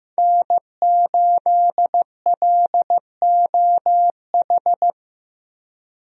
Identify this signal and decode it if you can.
Morse code, heard as a keyed tone, about 15 wpm, 700 Hz. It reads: N8LOH
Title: USA Call Signs - 5 Characters - Repeated 2x - 15wpm